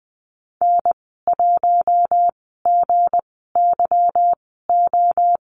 Morse code, heard as a keyed tone, about 20 wpm, spaced as for 14 wpm, 700 Hz. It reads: N1GYO
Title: 238 Lesson 38g - Letter q - Call Signs 20wpm